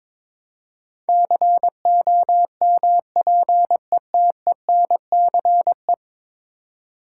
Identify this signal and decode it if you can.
Morse code, heard as a keyed tone, about 22 wpm, 700 Hz. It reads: COMPETENCE
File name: practice / 22wpm / 10 Letter Words - First 3 Letters Given - 22wpm